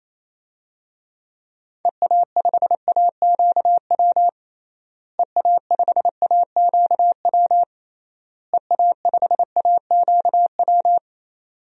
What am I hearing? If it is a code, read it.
Morse code, heard as a keyed tone, about 28 wpm, 700 Hz. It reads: EA5AQW EA5AQW EA5AQW
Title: DXCC-100 Call Signs - 6 Characters - Repeated 3x - 28wpm